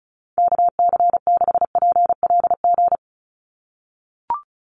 Morse code, heard as a keyed tone, about 36 wpm, 700 Hz. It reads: KC6PLG
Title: USA Call Signs 35wpm